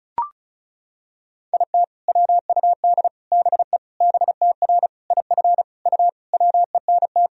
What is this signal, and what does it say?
Morse code, heard as a keyed tone, about 36 wpm, 700 Hz. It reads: IT WUD BE BTR IF U WENT
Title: Sentences from Top 400 Words and Abbreviations 35wpm